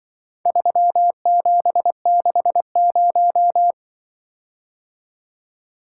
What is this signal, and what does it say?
Morse code, heard as a keyed tone, about 23 wpm, 700 Hz. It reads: 3760
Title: Speed-Racing - Four Letter-Number Combinations - 20wpm